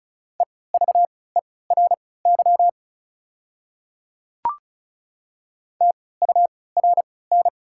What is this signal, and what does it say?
Morse code, heard as a keyed tone, about 36 wpm, spaced as for 20 wpm, 700 Hz. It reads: EVERY TURN
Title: 3x Character Spacing - Top 500 Words - 35wpm